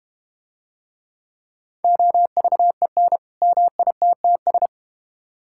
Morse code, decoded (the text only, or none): OVEN MITTS